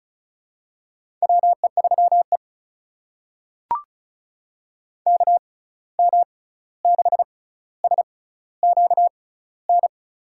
Morse code, heard as a keyed tone, about 34 wpm, spaced as for 16 wpm, 700 Hz. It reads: WE3E KM6SQN
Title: USA Call Signs - 6x Character Spacing - 35wpm